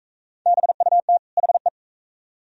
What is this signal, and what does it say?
Morse code, heard as a keyed tone, about 40 wpm, 690 Hz. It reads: BUT HE